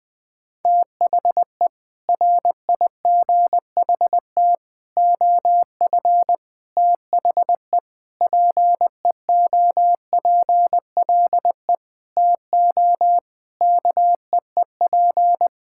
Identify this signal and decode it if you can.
Morse code, heard as a keyed tone, about 20 wpm, 700 Hz. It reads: THE RIGHT OF THE PEOPLE TO KEEP
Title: Sentences with 9-Word Phrases - 20wpm